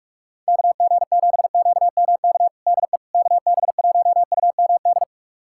Morse code, heard as a keyed tone, about 47 wpm, 700 Hz. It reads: KG7QMK DE KB1UMD
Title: USA Callsign DE Callsign - 6 Characters - 45wpm